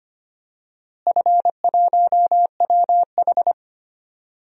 Morse code, heard as a keyed tone, about 25 wpm, 700 Hz. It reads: F1WH
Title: DXCC-100 Call Signs - 4 Characters - Repeated 3x - 25wpm